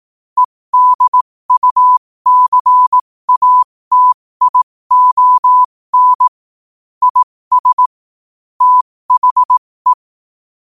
Morse code, heard as a keyed tone, about 18 wpm, spaced as for 15 wpm, 1000 Hz. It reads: EDUCATION IS THE
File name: qotdfiles / 15wpm